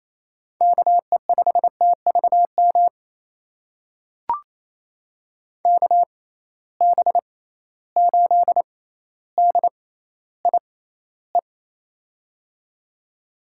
Morse code, heard as a keyed tone, about 28 wpm, spaced as for 14 wpm, 700 Hz. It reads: KE5TVM KB8DIE
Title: USA Call Signs - 6 Characters - 6x Character Spacing - 28wpm